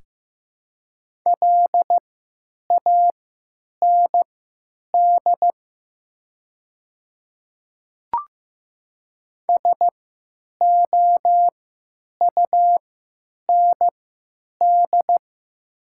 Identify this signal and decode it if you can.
Morse code, heard as a keyed tone, about 15 wpm, spaced as for 9 wpm, 700 Hz. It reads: LAND SOUND
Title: 3x Character Spacing - Top 200 Words - 15wpm